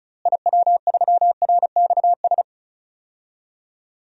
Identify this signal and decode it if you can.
Morse code, heard as a keyed tone, about 36 wpm, 700 Hz. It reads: IW3RXS
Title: DXCC-100 Call Signs - 6 Characters - Repeated 3x - 35wpm